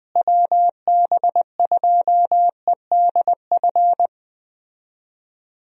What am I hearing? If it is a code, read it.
Morse code, heard as a keyed tone, about 21 wpm, 700 Hz. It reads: WB2EDF
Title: USA Call Signs - 6 Characters - Repeated 2x - 20wpm